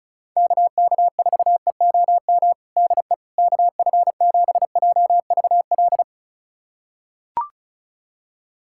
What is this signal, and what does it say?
Morse code, heard as a keyed tone, about 36 wpm, 700 Hz. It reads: KK4EOM DE KF7JVL
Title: USA Callsign DE Callsign - 6 Characters - 35wpm